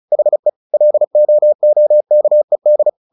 Morse code, heard as a keyed tone, about 34 wpm, 600 Hz. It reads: HE LOOKED
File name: the old man and the sea 2 35wpm